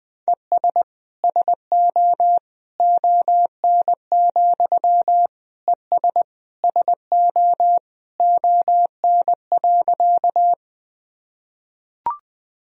Morse code, heard as a keyed tone, about 20 wpm, 700 Hz. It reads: ES SO ON, ES SO ON.
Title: Sentences from Top 100 Words and Abbreviations 20wpm